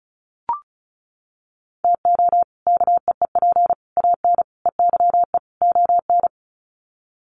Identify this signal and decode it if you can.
Morse code, heard as a keyed tone, about 34 wpm, 700 Hz. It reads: TO KEEP AN EYE ON
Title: Sets of 5 Words Encore 35wpm